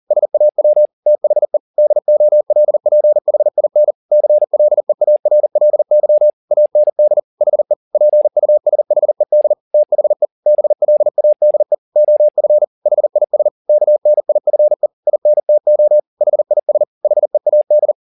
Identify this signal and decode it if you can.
Morse code, heard as a keyed tone, about 40 wpm, 600 Hz. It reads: SAW THE DOLPHIN CLEARLY AND HE PUSHED THE BLADE OF HIS KNIFE INTO HIS HEAD
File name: the old man and the sea 2 40wpm